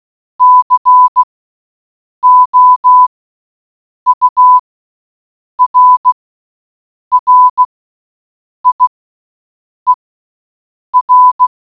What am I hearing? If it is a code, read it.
Morse code, heard as a keyed tone, about 16 wpm, spaced as for 7 wpm, 1000 Hz. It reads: COURRIER